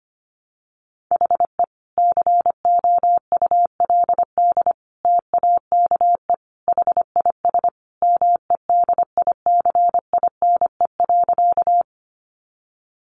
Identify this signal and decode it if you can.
Morse code, heard as a keyed tone, about 25 wpm, 700 Hz. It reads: HE COULD TAKE HIS MEDICINE.
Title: Sentences from Top 900 Words 25wpm